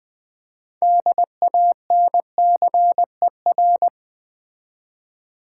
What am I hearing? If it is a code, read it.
Morse code, heard as a keyed tone, about 20 wpm, 700 Hz. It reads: DANCER
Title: 6 Letter Words - First 3 Letters Given - 20wpm